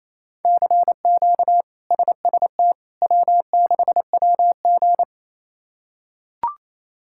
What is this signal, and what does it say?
Morse code, heard as a keyed tone, about 28 wpm, 700 Hz. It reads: CQ SST W6WG
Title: K1USN SST QSO Practice - 28wpm